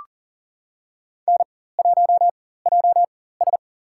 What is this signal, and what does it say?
Morse code, heard as a keyed tone, about 40 wpm, spaced as for 19 wpm, 700 Hz. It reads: N1JS
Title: USA Call Signs - 4 Characters - 4x Character Spacing - 40wpm